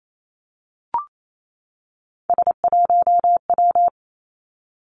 Morse code, heard as a keyed tone, about 28 wpm, 700 Hz. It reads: S1W
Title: Three Challenging Letter-Number Combinations 28wpm